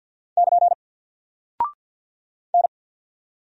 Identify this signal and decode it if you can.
Morse code, heard as a keyed tone, about 50 wpm, 700 Hz. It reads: <KN> N
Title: Single Character (Letter:Number:Punctuation:Prosign) Alt 50wpm